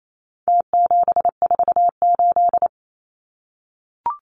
Morse code, heard as a keyed tone, about 28 wpm, 700 Hz. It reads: T748
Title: Numbers - Contest Style - 4-Digit 28wpm